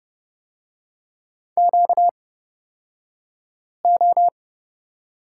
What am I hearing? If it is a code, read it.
Morse code, heard as a keyed tone, about 30 wpm, spaced as for 5 wpm, 700 Hz. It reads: QO
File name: Lesson 38d - Letter q - Two Characters 30wpm